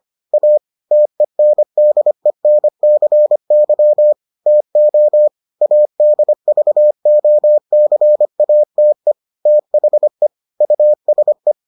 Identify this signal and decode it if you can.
Morse code, heard as a keyed tone, about 25 wpm, 600 Hz. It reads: A TENDENCY TO ADVOCATE THE USE